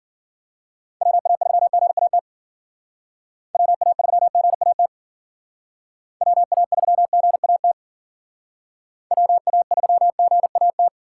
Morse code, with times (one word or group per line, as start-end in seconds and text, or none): 1.0-2.2 WA3GAT
3.5-4.9 WA3GAT
6.2-7.7 WA3GAT
9.1-10.9 WA3GAT